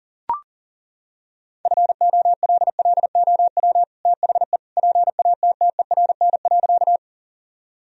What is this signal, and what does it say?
Morse code, heard as a keyed tone, about 40 wpm, 700 Hz. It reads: FOLLOW THE PATTERN.
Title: Sentences from Top 1000 Words - Review Part 2 - 40wpm